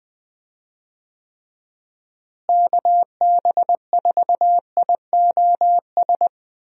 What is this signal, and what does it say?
Morse code, heard as a keyed tone, about 20 wpm, 700 Hz. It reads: KB4IOS
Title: USA Call Signs - 6 Characters - Repeated 2x - 20wpm